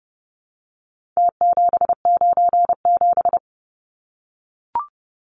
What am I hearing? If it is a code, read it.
Morse code, heard as a keyed tone, about 30 wpm, 700 Hz. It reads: T797
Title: Numbers - Contest Style - 4-Digit 30wpm